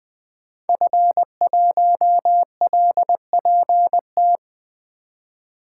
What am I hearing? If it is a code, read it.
Morse code, heard as a keyed tone, about 20 wpm, 700 Hz. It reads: F1LPT
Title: 2023 POTA Activator Call Signs - 20wpm